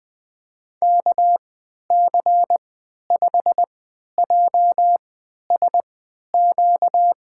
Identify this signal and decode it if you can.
Morse code, heard as a keyed tone, about 20 wpm, spaced as for 11 wpm, 700 Hz. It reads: KC5JSQ
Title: USA Call Signs - 6 Characters - 3x Character Spacing - 20wpm